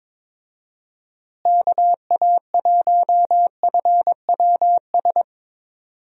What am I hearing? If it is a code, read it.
Morse code, heard as a keyed tone, about 22 wpm, 700 Hz. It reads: KA1FWS
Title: USA Call Signs - 6 Characters - Repeated 2x - 22wpm